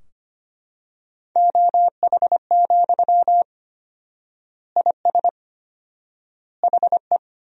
Morse code, heard as a keyed tone, about 25 wpm, 700 Hz. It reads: OH, IS HE
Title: 4x Word-Spacing - Sentences from Top 500 Words - 25wpm